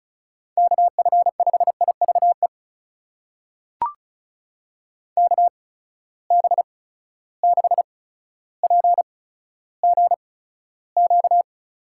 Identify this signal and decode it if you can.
Morse code, heard as a keyed tone, about 36 wpm, spaced as for 15 wpm, 700 Hz. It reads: KF5IVE KB6PGQ
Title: USA Call Signs - 6 Characters - 8x Character Spacing - 35wpm